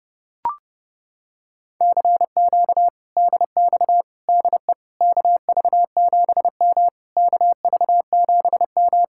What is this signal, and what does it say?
Morse code, heard as a keyed tone, about 29 wpm, 700 Hz. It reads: CQ DX DE KV7M KV7M